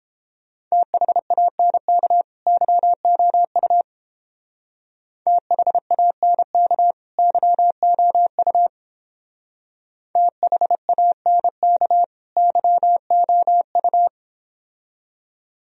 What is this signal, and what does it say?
Morse code, heard as a keyed tone, about 32 wpm, 700 Hz. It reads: THANK YOU THANK YOU THANK YOU